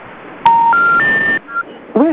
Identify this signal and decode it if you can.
Special information tone.